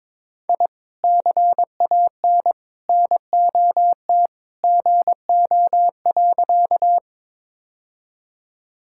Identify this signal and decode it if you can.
Morse code, heard as a keyed tone, about 22 wpm, 700 Hz. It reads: I CAN NOT GO.
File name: practice / 22wpm / Sentences with 2-Word Phrases - 22wpm